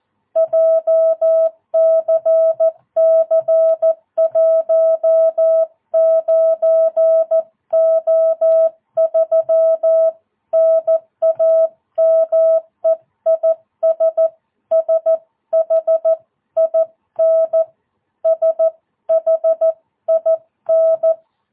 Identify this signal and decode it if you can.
Morse code, heard as a keyed tone, about 14 wpm, 660 Hz. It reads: JCC 19O3 NAME IS SHIN SHIN
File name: jh8cbh / 2022cw